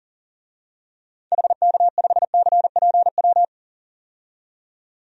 Morse code, heard as a keyed tone, about 40 wpm, 700 Hz. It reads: HK5CPW
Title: DXCC-100 Call Signs - 6 Characters - 40wpm